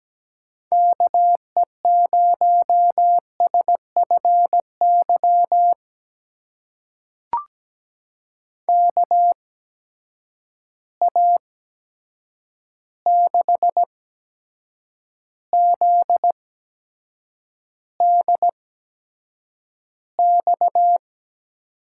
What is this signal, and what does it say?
Morse code, heard as a keyed tone, about 17 wpm, spaced as for 7 wpm, 700 Hz. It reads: KE0SFY KA6ZDX
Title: USA Call Signs - 6 Characters - 8x Character Spacing - 17wpm